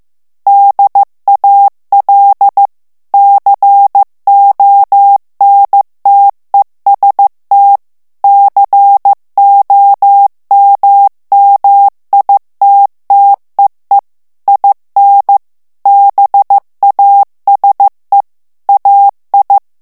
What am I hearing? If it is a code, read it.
Morse code, heard as a keyed tone, about 15 wpm, 800 Hz. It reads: DAL CONTEST COMMITTEE IN BASE AI
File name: morse mp3 / ITALIEN texte clair 2 vitesse 14 mots minute